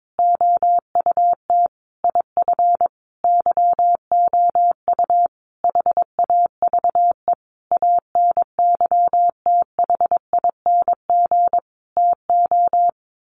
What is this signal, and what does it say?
Morse code, heard as a keyed tone, about 22 wpm, 700 Hz. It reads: OUT IF YOU HAVE ANYTHING TO
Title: Sentences from Top 1300 Words 22wpm